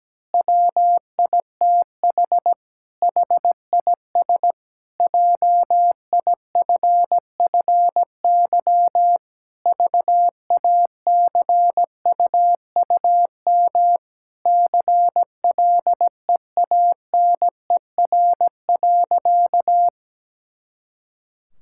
Morse code, heard as a keyed tone, about 17 wpm, 700 Hz. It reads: WITH HIS JIFFY VACUUM CLEANER.
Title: Pangram Challenge v2 - 17wpm